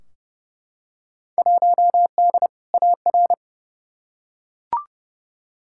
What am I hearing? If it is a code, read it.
Morse code, heard as a keyed tone, about 29 wpm, 700 Hz. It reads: 1D AR
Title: Field Day Exchanges Encore - 30wpm